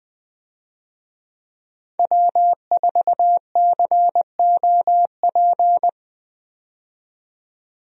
Morse code, heard as a keyed tone, about 20 wpm, 700 Hz. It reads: W4COP
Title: USA Call Signs - 5 Characters - 20wpm